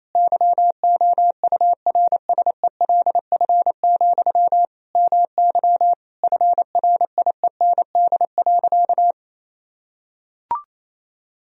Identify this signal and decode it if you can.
Morse code, heard as a keyed tone, about 28 wpm, 700 Hz. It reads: YOURSELF, MY FRIEND.
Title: Sentences from Top 1000 Words - Review Part 3 - 28wpm